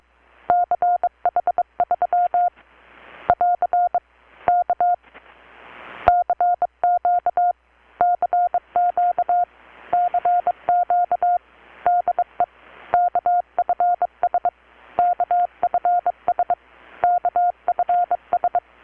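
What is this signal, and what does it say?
Morse code, heard as a keyed tone, about 23 wpm, 720 Hz. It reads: CH3 <AR> K CQ CQ CQ DE KFS KFS KFS
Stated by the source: Morse from KPH's Night of Nights XX maritime radiotelegraphy event, received in Chiapas, Mexico about 3,800 km away on a Kenwood R-600 with a grounded 10 m long-wire antenna.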